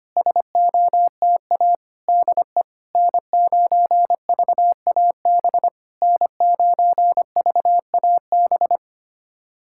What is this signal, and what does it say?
Morse code, heard as a keyed tone, about 25 wpm, 700 Hz. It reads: SOTA DE N9VAB N9VAB
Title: SOTA QSO Practice 25wpm